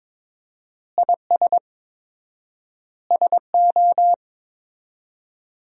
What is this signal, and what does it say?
Morse code, heard as a keyed tone, about 22 wpm, 700 Hz. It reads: IS SO
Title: 4x Word-Spacing - Sentences from Top 500 Words - 22wpm